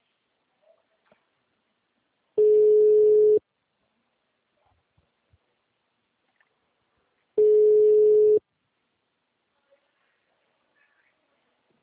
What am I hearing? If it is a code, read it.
Ringback tone (Europe, 425 Hz).